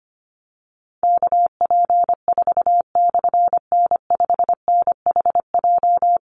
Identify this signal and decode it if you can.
Morse code, heard as a keyed tone, about 25 wpm, 700 Hz. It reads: KP4/N5NHJ
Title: Call Signs - 25wpm